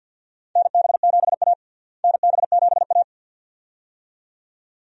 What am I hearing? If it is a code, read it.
Morse code, heard as a keyed tone, about 52 wpm, 700 Hz. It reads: NB7A NB7A
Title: USA Call Signs - 4 Characters - Repeated 2x - 50wpm